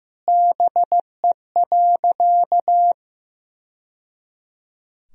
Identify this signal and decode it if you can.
Morse code, heard as a keyed tone, about 15 wpm, 700 Hz. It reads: BE.
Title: Sentences from Top 1000 Words - Review Part 3 - 15wpm